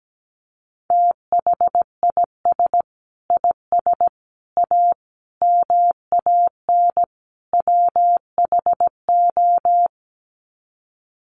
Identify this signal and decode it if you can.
Morse code, heard as a keyed tone, about 17 wpm, 700 Hz. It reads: THIS IS A MAN WHO
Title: Sets of 5 Words Encore2 17wpm